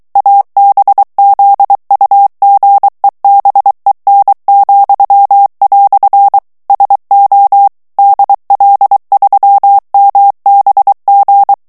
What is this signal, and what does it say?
Morse code, heard as a keyed tone, about 23 wpm, 800 Hz. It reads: ABZUGEBEN," SO DL3MBG
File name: morse mp3 / ALLEMAND texte clair 1 vitesse 22 mots minute